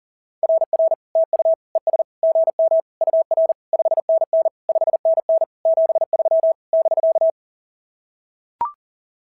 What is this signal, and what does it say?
Morse code, heard as a keyed tone, about 41 wpm, 690 Hz. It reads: RR TU ES GM UR 5NN 5NN 73 <BK>
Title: SOTA QSO Practice 40wpm